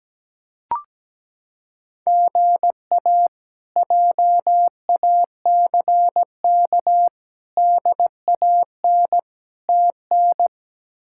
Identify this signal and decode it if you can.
Morse code, heard as a keyed tone, about 17 wpm, 700 Hz. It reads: GA JACK DAN TN
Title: K1USN SST QSO Practice - 17wpm